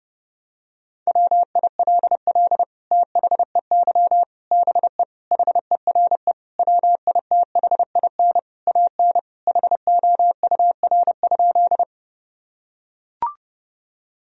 Morse code, heard as a keyed tone, about 32 wpm, 700 Hz. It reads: WILL THEY BE HERE WITHIN AN HOUR?
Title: Sentences from Top 1000 Words - Review Part 4 - 30wpm